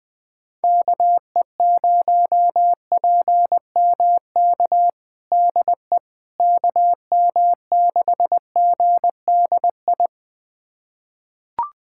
Morse code, heard as a keyed tone, about 21 wpm, 700 Hz. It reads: KE0PMK DE KM6GDI
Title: USA Callsign DE Callsign - 6 Characters - 20wpm